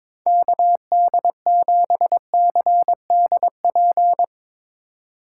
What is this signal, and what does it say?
Morse code, heard as a keyed tone, about 24 wpm, 700 Hz. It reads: KD7CDP
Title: Speed Racing - USA 2x3 Call Signs 22wpm